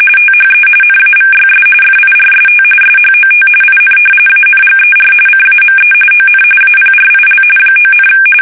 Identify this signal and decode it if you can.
RTTY, about 45.45 baud. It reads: THE NAVYS FIRST SHOT IN ANGER CAME ON APRIL 9,
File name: fsk850